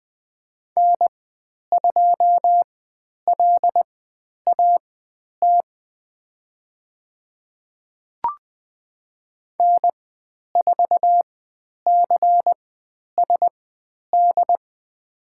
Morse code, heard as a keyed tone, about 20 wpm, spaced as for 10 wpm, 700 Hz. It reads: N2LAT N4CSD
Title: 109 Lesson 18g - Letter m - Call Signs 20wpm